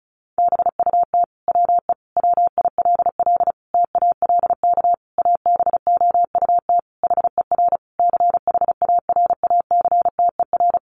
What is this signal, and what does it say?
Morse code, heard as a keyed tone, about 34 wpm, 700 Hz. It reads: BUT WE WILL TALK ABOUT HER CHARACTER
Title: Sentences from Top 1200 Words 35wpm